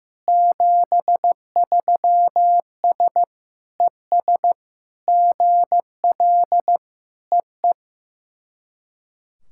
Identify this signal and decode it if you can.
Morse code, heard as a keyed tone, about 16 wpm, 700 Hz. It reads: 73S ES GL EE
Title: SOTA QSO Practice 15wpm